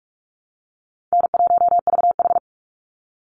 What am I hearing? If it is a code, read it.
Morse code, heard as a keyed tone, about 44 wpm, 700 Hz. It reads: N1VH
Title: Call Signs Repeated 2x - 45wpm